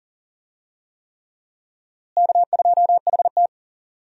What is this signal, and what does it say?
Morse code, heard as a keyed tone, about 38 wpm, 700 Hz. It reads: K2HT
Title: USA Call Signs - 4 Characters - Repeated 3x - 40wpm